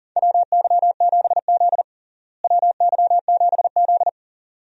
Morse code, heard as a keyed tone, about 40 wpm, 700 Hz. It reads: WY7Z WY7Z
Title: USA Call Signs - 4 Characters - Repeated 2x - 40wpm